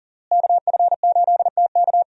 Morse code, heard as a keyed tone, about 40 wpm, 690 Hz. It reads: KF8TK
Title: USA Call Signs - 5 Characters - Repeated 3x - 40wpm